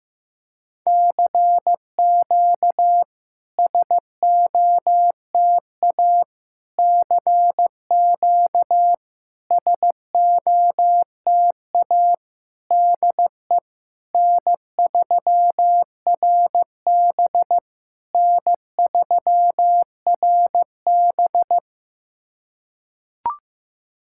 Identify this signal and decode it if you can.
Morse code, heard as a keyed tone, about 15 wpm, 700 Hz. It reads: CQ SOTA CQ SOTA DE N3RB N3RB
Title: SOTA QSO Practice 15wpm